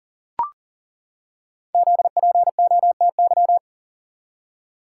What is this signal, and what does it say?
Morse code, heard as a keyed tone, about 41 wpm, 690 Hz. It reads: ZPOTY
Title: Code Groups - 5 Letters - 40wpm